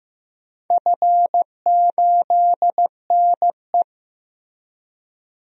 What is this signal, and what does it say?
Morse code, heard as a keyed tone, about 15 wpm, 700 Hz. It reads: F8NE
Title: DXCC-100 Call Signs - 4 Characters - 15wpm